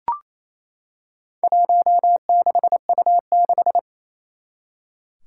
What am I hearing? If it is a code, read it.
Morse code, heard as a keyed tone, about 28 wpm, 700 Hz. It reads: 16U6
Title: Four Challenging Letter-Number Combinations 28wpm